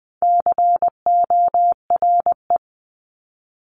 Morse code, heard as a keyed tone, about 20 wpm, 700 Hz. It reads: CORE
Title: 260 Lesson 42e - Words 20-20wpm